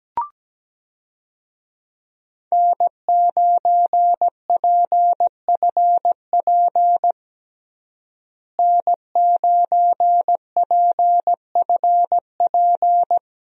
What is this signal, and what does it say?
Morse code, heard as a keyed tone, about 17 wpm, 700 Hz. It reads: N9PFP N9PFP
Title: USA Call Signs - 5 Characters - Repeated 2x - 17wpm